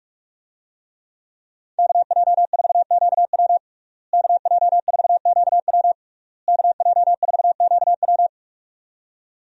Morse code, heard as a keyed tone, about 47 wpm, 700 Hz. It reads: KJ4QW KJ4QW KJ4QW
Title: USA Call Signs - 5 Characters - Repeated 3x - 45wpm